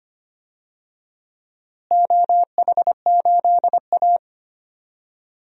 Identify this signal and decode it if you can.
Morse code, heard as a keyed tone, about 25 wpm, 700 Hz. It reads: OH8A
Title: DXCC-100 Call Signs - 4 Characters - Repeated 3x - 25wpm